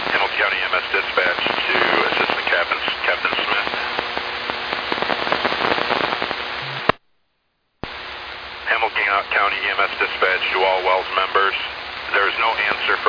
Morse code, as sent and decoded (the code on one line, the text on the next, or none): none